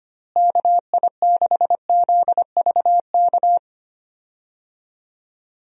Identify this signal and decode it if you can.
Morse code, heard as a keyed tone, about 25 wpm, 700 Hz. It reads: KI6ZVK
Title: USA Call Signs - 6 Characters - Repeated 3x - 25wpm